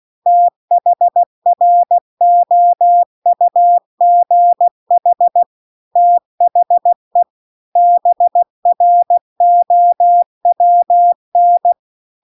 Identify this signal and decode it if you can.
Morse code, heard as a keyed tone, about 16 wpm, 700 Hz. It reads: THROUGH THE BROWN